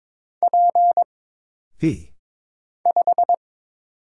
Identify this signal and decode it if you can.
Morse code, heard as a keyed tone, about 22 wpm, 700 Hz. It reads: P 5
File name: Single Character (Letter:Number:Punctuation:Prosign) - Mind-Melt - 22wpm